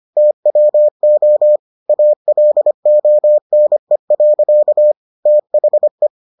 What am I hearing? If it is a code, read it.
Morse code, heard as a keyed tone, about 25 wpm, 600 Hz. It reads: TWO ALONE. THE